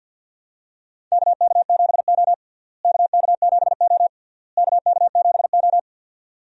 Morse code, heard as a keyed tone, about 47 wpm, 700 Hz. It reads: KK7O KK7O KK7O
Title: USA Call Signs - 4 Characters - Repeated 3x - 50wpm